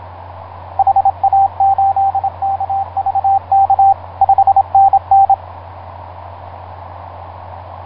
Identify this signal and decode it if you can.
Morse code, heard as a keyed tone, about 27 wpm, 780 Hz, with radio noise. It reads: HA8KVK 5NN
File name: vk9x-g6ay ha7lw etc 10m cw 23feb2010 0836